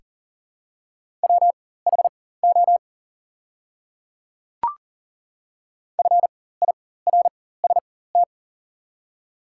Morse code, heard as a keyed tone, about 38 wpm, spaced as for 18 wpm, 700 Hz. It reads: WHO FIRST
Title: 4x Character Spacing - Top 300 Words - 40wpm